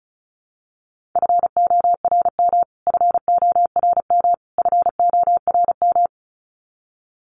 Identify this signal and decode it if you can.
Morse code, heard as a keyed tone, about 34 wpm, 700 Hz. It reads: FORM FORM FORM
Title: Top 500 Words Repeated 35wpm